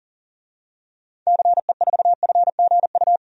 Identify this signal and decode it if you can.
Morse code, heard as a keyed tone, about 40 wpm, 700 Hz. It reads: CE4FGU
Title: DXCC-100 Call Signs - 6 Characters - 40wpm